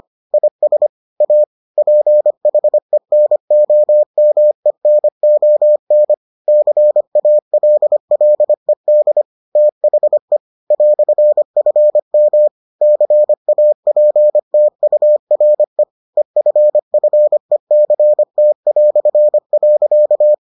Morse code, heard as a keyed tone, about 25 wpm, 600 Hz. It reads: IS A PHENOMENON CALLED THE "FM CAPTURE EFFECT".